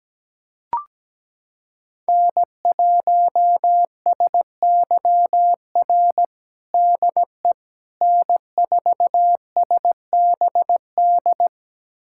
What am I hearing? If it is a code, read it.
Morse code, heard as a keyed tone, about 17 wpm, 700 Hz. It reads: N1SYR DE N4SBD
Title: USA Callsign DE Callsign - 5 Characters - 17wpm